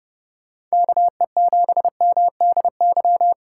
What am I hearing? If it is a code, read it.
Morse code, heard as a keyed tone, about 39 wpm, 700 Hz. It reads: KE7MDY